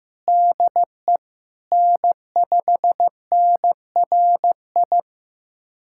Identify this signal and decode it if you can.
Morse code, heard as a keyed tone, about 15 wpm, 700 Hz. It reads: DE N5NRI
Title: USA Callsign DE Callsign 15wpm